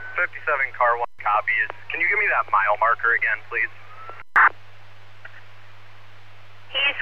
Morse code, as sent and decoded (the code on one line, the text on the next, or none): none